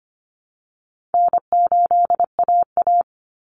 Morse code, heard as a keyed tone, about 25 wpm, 700 Hz. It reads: N8AA
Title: Call Signs Repeated 3x - 25wpm